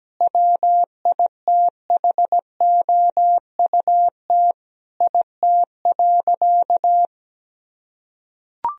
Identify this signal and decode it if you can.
Morse code, heard as a keyed tone, about 17 wpm, 700 Hz. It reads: WITHOUT IT.
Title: Sentences from Top 1000 Words - Review Part 3 - 17wpm